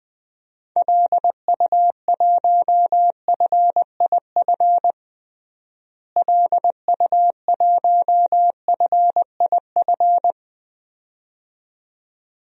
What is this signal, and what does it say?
Morse code, heard as a keyed tone, about 20 wpm, 700 Hz. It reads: LU1FIF LU1FIF
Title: DXCC-100 Call Signs - 6 Characters - Repeated 2x - 20wpm